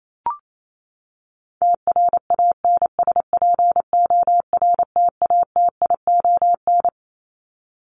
Morse code, heard as a keyed tone, about 28 wpm, 700 Hz. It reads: TRANSPORTATION
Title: Top 1501-1750 Words 28wpm